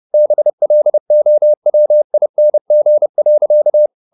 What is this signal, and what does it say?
Morse code, heard as a keyed tone, about 30 wpm, 600 Hz. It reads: BLOWING.
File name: the old man and the sea / the old man and the sea 1 30wpm